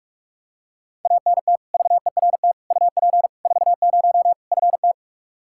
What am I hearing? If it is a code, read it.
Morse code, heard as a keyed tone, about 44 wpm, 700 Hz. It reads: ANT VERT UP 40 FT
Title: Practice QSO - Exchange 2 - Info - 45wpm